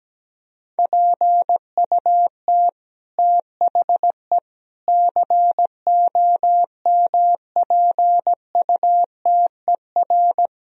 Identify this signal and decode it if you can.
Morse code, heard as a keyed tone, about 17 wpm, 700 Hz. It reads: PUT THE COMPUTER
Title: Sentences from Top 1000 Words - Review Part 3 - 17wpm